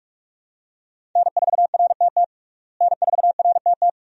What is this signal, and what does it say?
Morse code, heard as a keyed tone, about 46 wpm, 700 Hz. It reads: N4RTT N4RTT
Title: USA Call Signs - 5 Characters - Repeated 2x - 45wpm